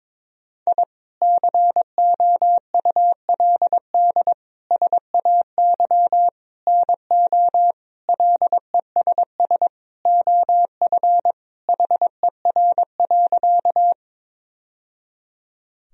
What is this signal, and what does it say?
Morse code, heard as a keyed tone, about 22 wpm, 700 Hz. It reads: I COULD SAY NO LESS OF HER.
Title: Sentences from Top 1000 Words - Review Part 2 - 22wpm